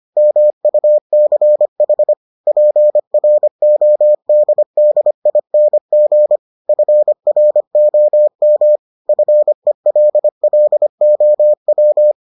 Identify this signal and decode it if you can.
Morse code, heard as a keyed tone, about 25 wpm, 600 Hz. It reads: MUCH PRODDING FROM FELLOW